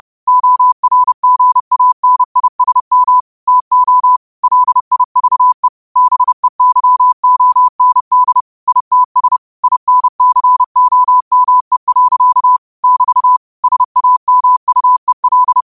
Morse code, heard as a keyed tone, about 30 wpm, 1000 Hz. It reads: ORGANISM TO LIVE BEYOND ITS INCOME. <BT> SAMUEL